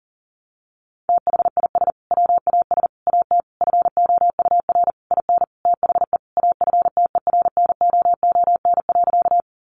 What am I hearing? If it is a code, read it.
Morse code, heard as a keyed tone, about 37 wpm, 700 Hz. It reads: THIS WAS AT FOUR IN THE AFTERNOON.
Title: Sentences from Top 1500 Words 40wpm